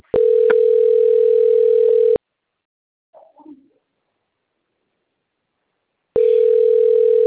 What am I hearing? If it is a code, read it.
Ringback tone (North America), 440 + 480 Hz.
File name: out- - - - - .53615